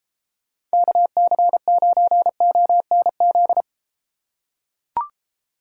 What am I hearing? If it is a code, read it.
Morse code, heard as a keyed tone, about 32 wpm, 700 Hz. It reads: KC9ONZ